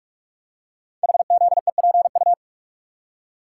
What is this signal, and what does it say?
Morse code, heard as a keyed tone, about 44 wpm, 700 Hz. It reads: HZEPU
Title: Code Groups - 5 Letters - 45wpm